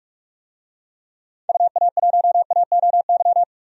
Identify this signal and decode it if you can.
Morse code, heard as a keyed tone, about 46 wpm, 700 Hz. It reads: UA1AOY
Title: DXCC-100 Call Signs - 6 Characters - 45wpm